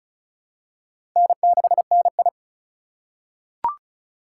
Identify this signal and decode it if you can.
Morse code, heard as a keyed tone, about 36 wpm, 700 Hz. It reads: N6NI
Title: USA Call Signs - 4 Characters - 3x Character Spacing - 35wpm